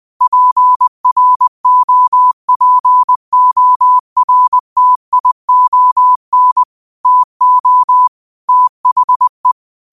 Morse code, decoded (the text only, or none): PROPORTION TO THE